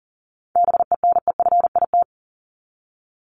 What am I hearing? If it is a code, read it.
Morse code, heard as a keyed tone, about 40 wpm, 700 Hz. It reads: BENEFIT
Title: Top 751-1000 Words 40wpm